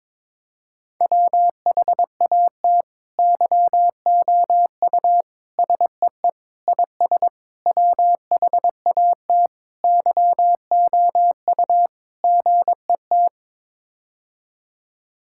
Morse code, heard as a keyed tone, about 22 wpm, 700 Hz. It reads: WHAT YOU SEE IS WHAT YOU GET
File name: practice / 22wpm / Sets of 7 Words - 22wpm